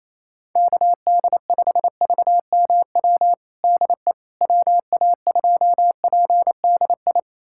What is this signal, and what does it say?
Morse code, heard as a keyed tone, about 28 wpm, 700 Hz. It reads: KD5VMW DE WA2PDI
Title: USA Callsign DE Callsign - 6 Characters - 28wpm